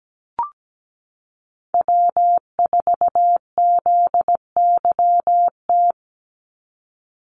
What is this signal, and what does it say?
Morse code, heard as a keyed tone, about 17 wpm, 700 Hz. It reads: W4ZYT
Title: Call Signs - 17wpm